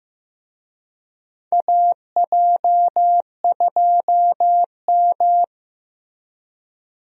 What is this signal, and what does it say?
Morse code, heard as a keyed tone, about 15 wpm, 700 Hz. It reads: AJ2M
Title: USA Call Signs - 4 Characters - Repeated 3x - 15wpm